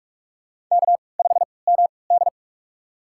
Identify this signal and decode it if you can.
Morse code, heard as a keyed tone, about 42 wpm, spaced as for 25 wpm, 700 Hz. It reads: K5MD
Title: USA Call Signs - 4 Characters - 3x Character Spacing - 45wpm